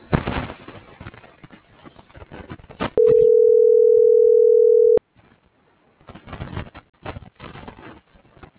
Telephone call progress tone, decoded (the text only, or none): ringback tone